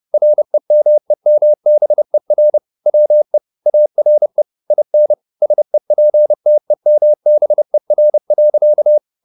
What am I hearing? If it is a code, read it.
Morse code, heard as a keyed tone, about 30 wpm, 600 Hz. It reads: REMEMBER WE ARE IN SEPTEMBER.
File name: the old man and the sea / the old man and the sea 1 30wpm